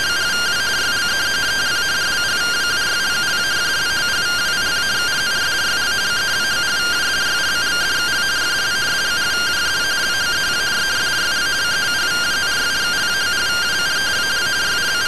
Two-tone teleprinter signal, about 170 Hz shift.